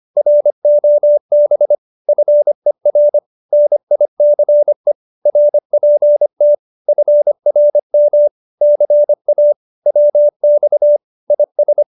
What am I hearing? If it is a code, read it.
Morse code, heard as a keyed tone, about 25 wpm, 590 Hz. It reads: ROB FER NICE RPT FRM CA WX IS
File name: QSO301 25